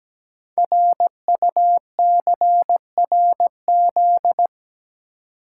Morse code, heard as a keyed tone, about 17 wpm, 700 Hz. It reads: RUCRZ
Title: Code Groups - 5 Letters - 17wpm